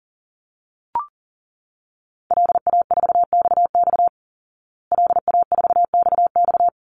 Morse code, heard as a keyed tone, about 40 wpm, 700 Hz. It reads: LA4XX LA4XX
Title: Call Signs Repeated 3x - 40wpm